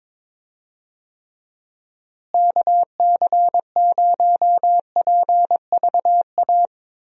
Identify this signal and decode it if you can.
Morse code, heard as a keyed tone, about 21 wpm, 700 Hz. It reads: KC0PVA
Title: USA Call Signs - 6 Characters - Repeated 2x - 22wpm